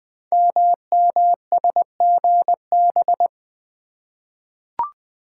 Morse code, heard as a keyed tone, about 20 wpm, 700 Hz. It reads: MMSGB
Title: Code Groups - 5 Letters - 20wpm